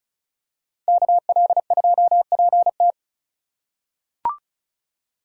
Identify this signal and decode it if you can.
Morse code, heard as a keyed tone, about 34 wpm, 700 Hz. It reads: KL2PT
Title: USA Call Signs - 4x Character Spacing - 35wpm